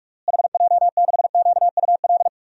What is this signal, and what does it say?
Morse code, heard as a keyed tone, about 41 wpm, 700 Hz. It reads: HJ6QUL
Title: DXCC-100 Call Signs - 6 Characters - Repeated 3x - 45wpm